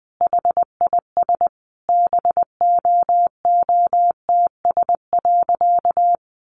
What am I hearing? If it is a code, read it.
Morse code, heard as a keyed tone, about 20 wpm, 700 Hz. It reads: HIS BOOTS.
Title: A Sound Of Thunder 20wpm